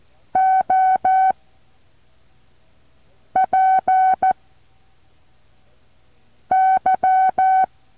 Morse code, heard as a keyed tone, about 14 wpm, spaced as for 4 wpm, 760 Hz. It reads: OPY
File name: intl morse code2